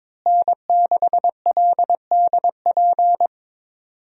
Morse code, heard as a keyed tone, about 22 wpm, 700 Hz. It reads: N6LDP
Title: USA Call Signs - 5 Characters - Repeated 2x - 22wpm